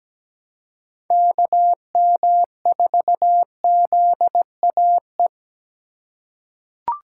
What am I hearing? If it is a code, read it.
Morse code, heard as a keyed tone, about 17 wpm, 700 Hz. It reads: KM4ZAE
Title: USA Call Signs - 2x Character Spacing - 17wpm